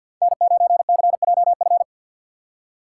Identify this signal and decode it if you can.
Morse code, heard as a keyed tone, about 49 wpm, 700 Hz. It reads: N9CJF
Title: USA Call Signs - 5 Characters - 4x Character Spacing - 50wpm